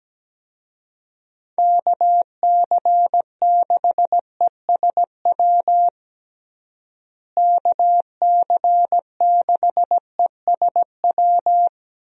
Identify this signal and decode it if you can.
Morse code, heard as a keyed tone, about 17 wpm, 700 Hz. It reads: KC6ESW KC6ESW
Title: USA Call Signs - 6 Characters - Repeated 2x - 17wpm